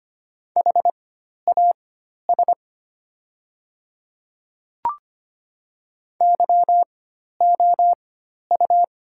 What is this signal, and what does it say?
Morse code, heard as a keyed tone, about 25 wpm, spaced as for 12 wpm, 700 Hz. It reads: HAS YOU
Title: 4x Character Spacing - Top 100 Words - 25wpm